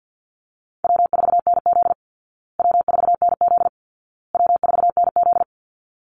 Morse code, heard as a keyed tone, about 50 wpm, 700 Hz. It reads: W4NZ W4NZ W4NZ
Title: Call Signs Repeated 2x - 50wpm